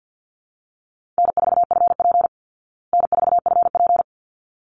Morse code, heard as a keyed tone, about 50 wpm, 700 Hz. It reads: N4FP N4FP
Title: Easy Call Signs Repeated 2x - 50wpm